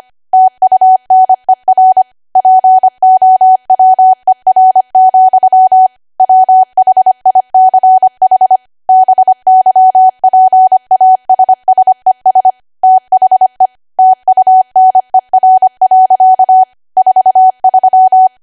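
Morse code, heard as a keyed tone, about 25 wpm, 750 Hz. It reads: TUNER POWER, WHICH BYPASSES THE TUNER. 43
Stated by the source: ARRL 25 WPM code practice archive.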